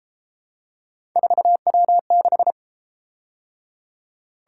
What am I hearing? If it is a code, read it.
Morse code, heard as a keyed tone, about 30 wpm, 700 Hz. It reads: 4W6